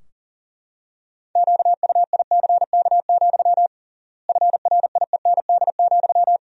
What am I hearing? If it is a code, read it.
Morse code, heard as a keyed tone, about 40 wpm, 700 Hz. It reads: QUICK, FRIEND,
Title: 3x Word-Spacing - Sentences from Top 500 Words - 40wpm